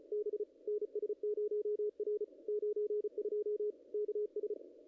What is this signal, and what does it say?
Morse code, heard as a keyed tone, about 36 wpm, 440 Hz, with radio noise. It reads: B NS0R 92 KS
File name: contests / ss-cw-k5tr-op-n5rz -0000